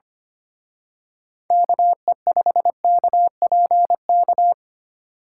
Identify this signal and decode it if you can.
Morse code, heard as a keyed tone, about 29 wpm, 700 Hz. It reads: KE5KPK
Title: Speed Racing - USA Call Signs 25wpm